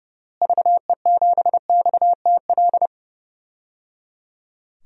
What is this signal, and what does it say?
Morse code, heard as a keyed tone, about 30 wpm, 700 Hz. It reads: VE7XTL
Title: DXCC-100 Call Signs - 6 Characters - Repeated 3x - 30wpm